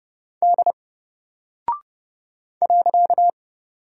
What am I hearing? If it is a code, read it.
Morse code, heard as a keyed tone, about 30 wpm, 700 Hz. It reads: D .
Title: Single Character (Letter:Number:Punctuation:Prosign) Alt 30wpm